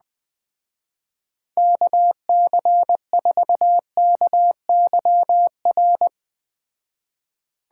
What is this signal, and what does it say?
Morse code, heard as a keyed tone, about 21 wpm, 700 Hz. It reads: KC4KYR